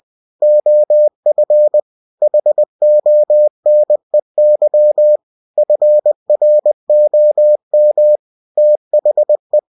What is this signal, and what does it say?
Morse code, heard as a keyed tone, about 20 wpm, 600 Hz. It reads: OF HONEY FROM THE
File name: cw-books / winnie-the-pooh / winnie-20wpm-page-0037